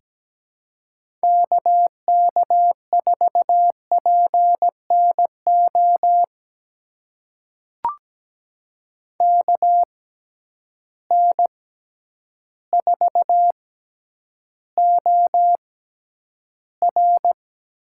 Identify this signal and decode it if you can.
Morse code, heard as a keyed tone, about 17 wpm, spaced as for 9 wpm, 700 Hz. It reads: KK4PNO KN4OR
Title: USA Call Signs - 6 Characters - 6x Character Spacing - 17wpm